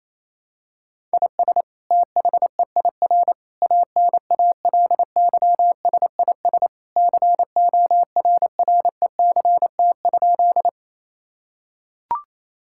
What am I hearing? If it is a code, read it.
Morse code, heard as a keyed tone, about 28 wpm, 700 Hz. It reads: IS THEIR ANALYSIS CORRECT?
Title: Sentences from Top 1000 Words - Review Part 5 - 28wpm